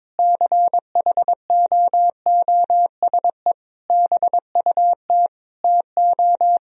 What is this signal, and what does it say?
Morse code, heard as a keyed tone, about 22 wpm, 700 Hz. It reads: CHOOSE BUT TO
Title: Sentences from Top 1000 Words - Review Part 4 - 22wpm